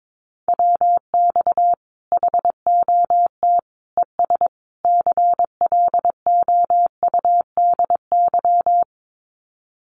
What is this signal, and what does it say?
Morse code, heard as a keyed tone, about 22 wpm, 700 Hz. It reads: WX HOT ES CLOUDY
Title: Common QSO Phrases 22wpm